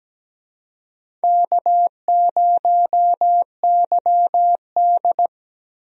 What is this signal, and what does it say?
Morse code, heard as a keyed tone, about 17 wpm, 700 Hz. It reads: K0YD
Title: USA Call Signs - 4 Characters - Repeated 3x - 17wpm